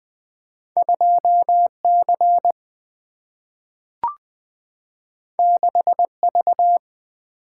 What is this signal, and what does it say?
Morse code, heard as a keyed tone, about 20 wpm, 700 Hz. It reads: 2 C 6 V
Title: Single-Letter Single-Number 20wpm